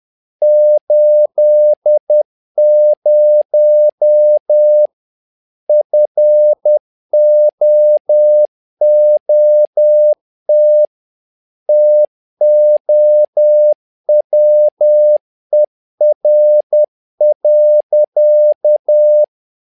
Morse code, heard as a keyed tone, about 10 wpm, 600 Hz. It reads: 80 FOOT TOWER.
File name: learning center / tower trouble / tower trouble 10wpm